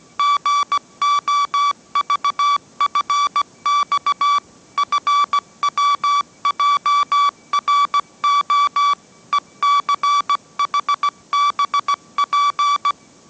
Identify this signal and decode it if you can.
Morse code, heard as a keyed tone, about 17 wpm, 1150 Hz. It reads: GOVFX FWJRO ECHBP
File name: mrx / mrx080 18wpm